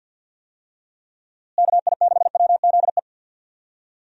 Morse code, heard as a keyed tone, about 50 wpm, 700 Hz. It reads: KI6WZE
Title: USA Call Signs - 6 Characters - Repeated 2x - 50wpm